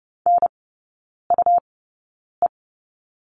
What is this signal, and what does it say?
Morse code, heard as a keyed tone, about 30 wpm, spaced as for 9 wpm, 700 Hz. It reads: NUE
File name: Lesson 17e - Letter u - Three Characters 30wpm